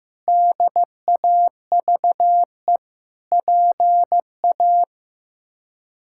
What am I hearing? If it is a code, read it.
Morse code, heard as a keyed tone, about 15 wpm, 700 Hz. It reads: DAVE PA
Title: K1USN SST QSO Practice - 15wpm